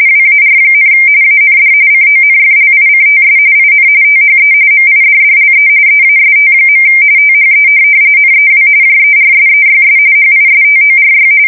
RTTY, about 45.45 baud. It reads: WAR, AND CONFLICTS IN THE MIDDLE EAST. THE NEW JERSEYS HISTORY
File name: fsk170 / USS-New-Jersey